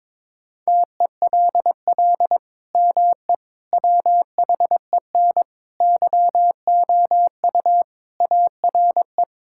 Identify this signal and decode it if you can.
Morse code, heard as a keyed tone, about 22 wpm, 700 Hz. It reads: TELL ME WHEN YOU ARE
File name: practice / 22wpm / Sentences with 2-Word Phrases - 22wpm